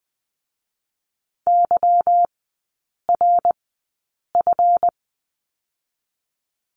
Morse code, heard as a keyed tone, about 20 wpm, 700 Hz. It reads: Y R F
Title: 198 Lesson 32e - Letter k - Three Characters 20wpm